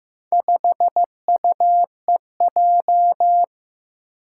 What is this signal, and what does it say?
Morse code, heard as a keyed tone, about 15 wpm, 700 Hz. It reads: 5UEJ
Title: Four Challenging Letter-Number Combinations 15wpm